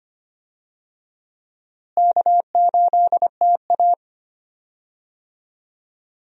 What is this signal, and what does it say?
Morse code, heard as a keyed tone, about 25 wpm, 700 Hz. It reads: K8TA
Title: USA Call Signs - 4 Characters - Repeated 2x - 25wpm